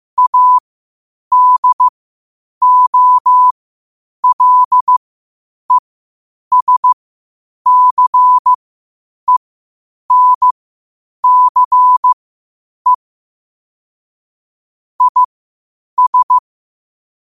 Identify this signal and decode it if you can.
Morse code, heard as a keyed tone, about 15 wpm, spaced as for 8 wpm, 1000 Hz. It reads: ADOLESCENCE IS
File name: qotdfiles / 8wpm